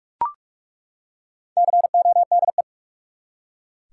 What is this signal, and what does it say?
Morse code, heard as a keyed tone, about 42 wpm, 700 Hz. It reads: CODE
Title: Four Letter Words - 45wpm